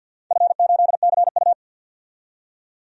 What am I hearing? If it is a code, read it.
Morse code, heard as a keyed tone, about 48 wpm, 700 Hz. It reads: F8CU
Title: DXCC-100 Call Signs - 4 Characters - 50wpm